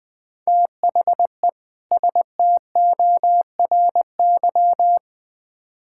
Morse code, heard as a keyed tone, about 20 wpm, 700 Hz. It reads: THE STORY
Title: Build A Phrase - 3-Word Phrases - 20wpm